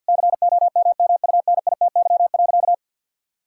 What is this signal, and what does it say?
Morse code, heard as a keyed tone, about 52 wpm, 700 Hz. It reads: COMMUNITY.
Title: Sentences from Top 1000 Words - Review Part 5 - 50wpm